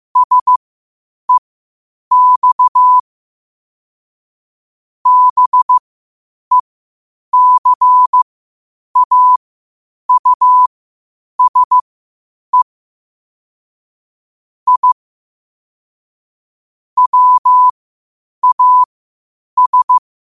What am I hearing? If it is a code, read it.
Morse code, heard as a keyed tone, about 15 wpm, spaced as for 8 wpm, 1000 Hz. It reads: SEX BECAUSE I WAS
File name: qotdfiles / 8wpm